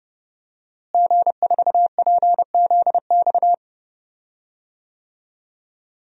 Morse code, heard as a keyed tone, about 31 wpm, 700 Hz. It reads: G4PZX
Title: DXCC-100 Call Signs - 5 Characters - Repeated 3x - 30wpm